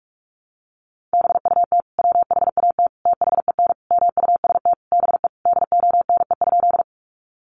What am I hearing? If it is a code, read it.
Morse code, heard as a keyed tone, about 44 wpm, 700 Hz. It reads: BUT WHAT THEN MUST BE DONE?
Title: Sentences from Top 500 Words 45wpm